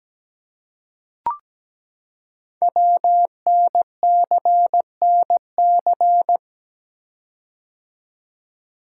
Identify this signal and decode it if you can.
Morse code, heard as a keyed tone, about 17 wpm, 700 Hz. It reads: WNCNC
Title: Code Groups - 5 Letters - 17wpm